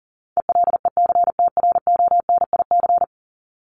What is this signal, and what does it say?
Morse code, heard as a keyed tone, about 40 wpm, 700 Hz. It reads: ELECTRONIC
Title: Top 1001-1250 Words 40wpm